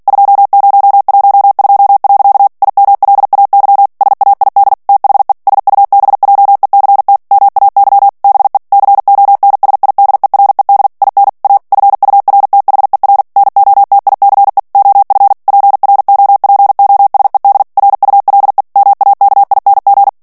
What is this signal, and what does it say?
Morse code, heard as a keyed tone, about 48 wpm, 800 Hz. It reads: 2012. IMLAY SAID THE SUBJECT MAY BE CONSIDERED IN A FURTHER NOTICE OF PROPOSED RULE MAKING
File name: ANGLAIS texte clair 3 vitesse 45 mots minute